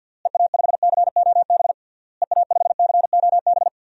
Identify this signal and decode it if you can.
Morse code, heard as a keyed tone, about 51 wpm, 700 Hz. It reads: EA5COB EA5COB
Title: DXCC-100 Call Signs - 6 Characters - Repeated 3x - 50wpm